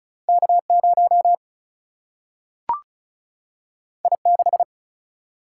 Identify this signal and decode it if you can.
Morse code, heard as a keyed tone, about 36 wpm, 700 Hz. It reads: K 0 I 6
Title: Single-Letter Single-Number 35wpm